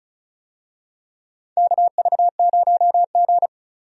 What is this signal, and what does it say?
Morse code, heard as a keyed tone, about 36 wpm, 700 Hz. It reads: KV0G
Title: USA Call Signs - 4 Characters - 35wpm